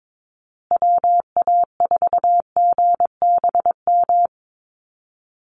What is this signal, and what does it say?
Morse code, heard as a keyed tone, about 22 wpm, 700 Hz. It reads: WA4GBM
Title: USA Call Signs Repeated 3x 22wpm